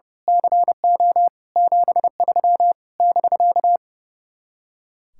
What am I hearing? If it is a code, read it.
Morse code, heard as a keyed tone, about 28 wpm, 700 Hz. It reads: CO 73 <BK>
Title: Speed-Racing Common QSO Phrases - 30wpm